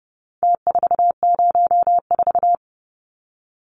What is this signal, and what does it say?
Morse code, heard as a keyed tone, about 30 wpm, 700 Hz. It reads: T404
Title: Numbers - Contest Style - 4-Digit 30wpm